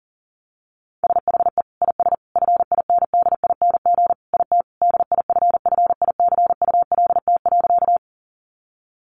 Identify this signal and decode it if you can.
Morse code, heard as a keyed tone, about 40 wpm, 700 Hz. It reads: SHE IS FINDING IT DIFFICULT.
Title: Sentences from Top 1100 Words 40wpm